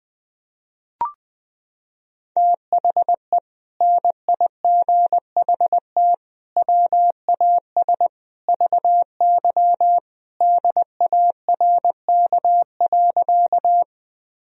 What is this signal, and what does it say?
Morse code, heard as a keyed tone, about 20 wpm, 700 Hz. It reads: THE NIGHT WAS VY DARK.
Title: Sentences from Top 500 Words and Abbreviations 20wpm